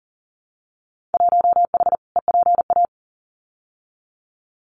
Morse code, heard as a keyed tone, about 40 wpm, 700 Hz. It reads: 1H EPA
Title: Winter Field Day Exchanges 40wpm